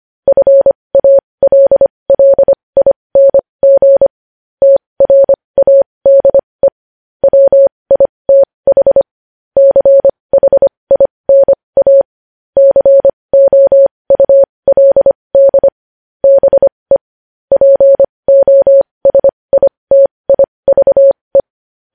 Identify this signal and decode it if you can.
Morse code, heard as a keyed tone, about 25 wpm, spaced as for 20 wpm, 550 Hz. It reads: FALLING TRADE WITH CHINA COULD BE POSITIVE
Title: News Headlines at 20